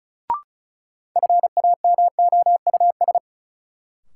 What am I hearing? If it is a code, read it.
Morse code, heard as a keyed tone, about 34 wpm, 700 Hz. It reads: FAMOUS
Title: Sentences from Top 1500 Words - New Word Drill - 35wpm